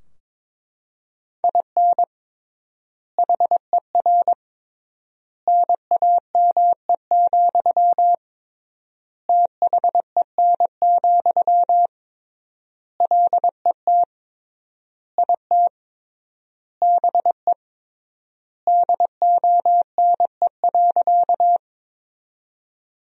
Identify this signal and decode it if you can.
Morse code, heard as a keyed tone, about 22 wpm, 700 Hz. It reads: IN HER NAME, THEN, LET IT BE DONE.
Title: 3x Word-Spacing - Sentences from Top 500 Words - 22wpm